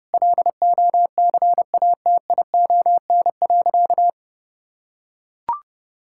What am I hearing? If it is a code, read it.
Morse code, heard as a keyed tone, about 30 wpm, 700 Hz. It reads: LOCATION.
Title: Sentences from Top 1000 Words - Review Part 6 - 30wpm